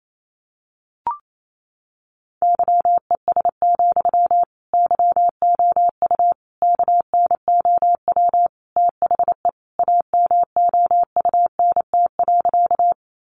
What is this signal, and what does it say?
Morse code, heard as a keyed tone, about 28 wpm, 700 Hz. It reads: YES, YOU KNOW THE AMOUNT.
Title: Sentences from Top 700 Words 28wpm